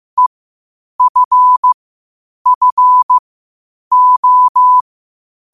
Morse code, heard as a keyed tone, about 15 wpm, spaced as for 8 wpm, 1000 Hz. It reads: EFFO
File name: qotdfiles / 8wpm